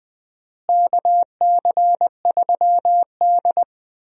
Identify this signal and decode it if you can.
Morse code, heard as a keyed tone, about 20 wpm, 700 Hz. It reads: KC3D
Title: USA Call Signs - 4 Characters - Repeated 3x - 20wpm